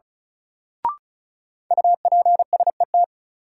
Morse code, heard as a keyed tone, about 36 wpm, 700 Hz. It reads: UPSET
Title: Sentences from Top 800 Words - New Word Drill - 35wpm